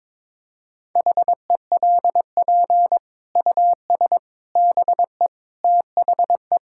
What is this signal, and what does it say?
Morse code, heard as a keyed tone, about 22 wpm, 700 Hz. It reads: HELP US BE THE
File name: Sets of 9 Words - 22wpm